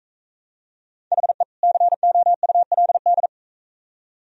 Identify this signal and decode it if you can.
Morse code, heard as a keyed tone, about 44 wpm, 700 Hz. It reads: HE COULD